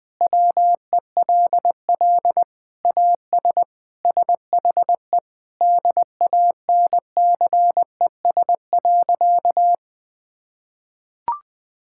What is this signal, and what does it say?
Morse code, heard as a keyed tone, about 20 wpm, 700 Hz. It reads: WELL AS SHE DANCES.
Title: Sentences with 3-Word Phrases - 20wpm